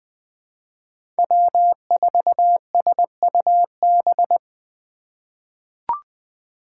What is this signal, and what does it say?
Morse code, heard as a keyed tone, about 20 wpm, 700 Hz. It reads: W4SUB
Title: USA Call Signs - 5 Characters - 2x Character Spacing - 20wpm